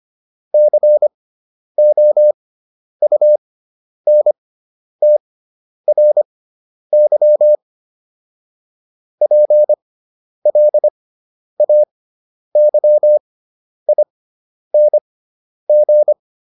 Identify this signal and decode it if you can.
Morse code, heard as a keyed tone, about 25 wpm, spaced as for 10 wpm, 600 Hz. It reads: COUNTRY PLAYING